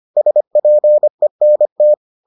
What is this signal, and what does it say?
Morse code, heard as a keyed tone, about 25 wpm, 600 Hz. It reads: SPENT